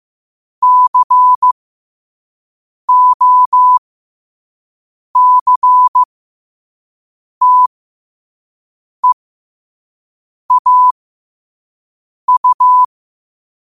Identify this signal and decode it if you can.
Morse code, heard as a keyed tone, about 15 wpm, spaced as for 5 wpm, 1000 Hz. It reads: COCTEAU
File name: qotdfiles / 5wpm